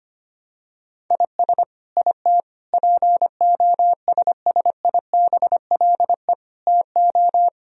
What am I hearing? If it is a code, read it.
Morse code, heard as a keyed tone, about 25 wpm, 700 Hz. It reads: IS IT POSSIBLE TO
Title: Sentences from Top 1000 Words - Review Part 3 - 25wpm